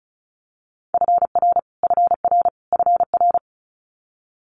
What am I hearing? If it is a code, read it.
Morse code, heard as a keyed tone, about 36 wpm, 700 Hz. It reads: FR FR FR
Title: Multi-letter Phonemes Repeated 3x 35wpm